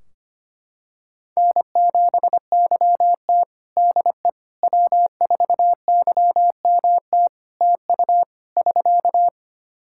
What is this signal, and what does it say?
Morse code, heard as a keyed tone, about 25 wpm, 700 Hz. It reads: N7YT DE W4YMT TU <SK>
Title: K1USN SST QSO Practice 25wpm